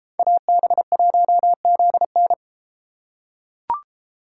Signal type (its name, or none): Morse code